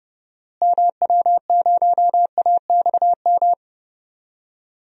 Morse code, heard as a keyed tone, about 30 wpm, 700 Hz. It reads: MW0AXM
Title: DXCC-100 Call Signs - 6 Characters - Repeated 2x - 30wpm